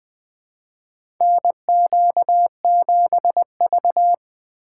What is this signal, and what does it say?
Morse code, heard as a keyed tone, about 20 wpm, 700 Hz. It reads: NQ7V
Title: USA Call Signs - 4 Characters - 3x Character Spacing - 20wpm